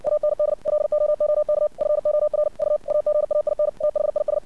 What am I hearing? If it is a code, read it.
Morse code, heard as a keyed tone, about 60 wpm, 600 Hz. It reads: AND LOOK FOR U AGAIN THIS
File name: archived / CRSnet